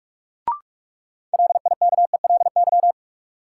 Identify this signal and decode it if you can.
Morse code, heard as a keyed tone, about 45 wpm, 700 Hz. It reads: LIKELY
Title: Sets of 2 Words - New Word Drill - 45wpm